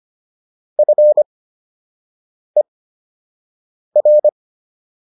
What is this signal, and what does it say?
Morse code, heard as a keyed tone, about 25 wpm, spaced as for 6 wpm, 600 Hz. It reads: FER